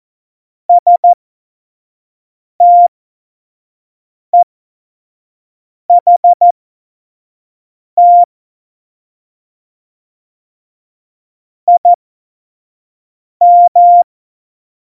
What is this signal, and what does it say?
Morse code, heard as a keyed tone, about 14 wpm, spaced as for 5 wpm, 700 Hz. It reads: STEHT IM
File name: Morsekurs / Witze / Witz 10 April 2018, 25 F70